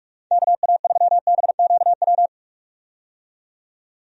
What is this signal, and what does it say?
Morse code, heard as a keyed tone, about 43 wpm, 700 Hz. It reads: KA3BQW
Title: USA Call Signs - 6 Characters - 45wpm